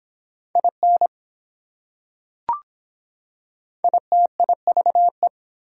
Morse code, heard as a keyed tone, about 24 wpm, 700 Hz. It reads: IN ITIVE